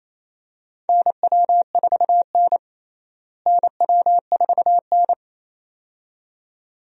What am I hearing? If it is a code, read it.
Morse code, heard as a keyed tone, about 28 wpm, 700 Hz. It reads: NW4N NW4N
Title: USA Call Signs - 4 Characters - Repeated 2x - 28wpm